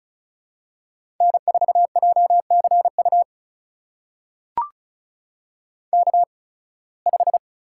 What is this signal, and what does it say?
Morse code, heard as a keyed tone, about 34 wpm, spaced as for 23 wpm, 700 Hz. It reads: N4JCU K5
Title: USA Call Signs - 5 Characters - 8x Character Spacing - 35wpm